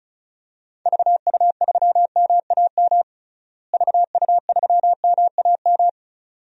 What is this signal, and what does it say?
Morse code, heard as a keyed tone, about 36 wpm, 700 Hz. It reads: VU3MAM VU3MAM
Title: DXCC-100 Call Signs - 6 Characters - Repeated 2x - 35wpm